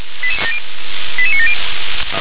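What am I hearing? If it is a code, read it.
Five-tone selective call (ZVEI), heard as quick runs of five tones, tones of about 70 ms. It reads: 88970
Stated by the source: ZVEI five-tone selective call received on SDR.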